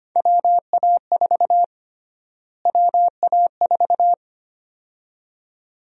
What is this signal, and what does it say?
Morse code, heard as a keyed tone, about 25 wpm, 700 Hz. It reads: WA4 WA4
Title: USA Call Sign Prefixes - Repeated 2x - 25wpm